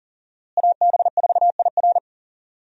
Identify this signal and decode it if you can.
Morse code, heard as a keyed tone, about 40 wpm, 700 Hz. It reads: AB4IR
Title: USA Call Signs - 5 Characters - Repeated 2x - 40wpm